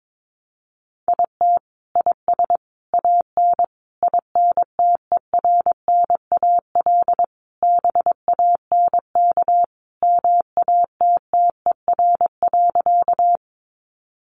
Morse code, heard as a keyed tone, about 22 wpm, 700 Hz. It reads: IT IS AN INTERNAL BANK MATTER.
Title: Sentences from Top 1400 Words 22wpm